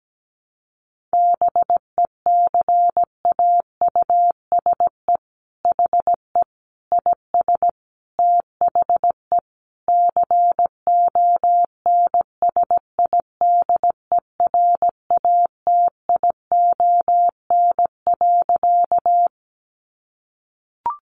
Morse code, heard as a keyed tone, about 17 wpm, 700 Hz. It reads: BECAUSE HE IS THE CONSIDERATION.
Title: Sentences from Top 1500 Words 17wpm